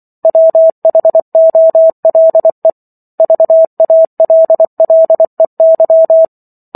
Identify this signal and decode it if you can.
Morse code, heard as a keyed tone, about 24 wpm, 650 Hz. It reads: WHOLE VALLEY